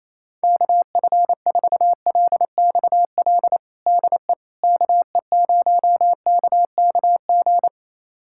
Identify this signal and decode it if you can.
Morse code, heard as a keyed tone, about 28 wpm, 700 Hz. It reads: KF4LXL DE KE0KKG
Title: USA Callsign DE Callsign - 6 Characters - 28wpm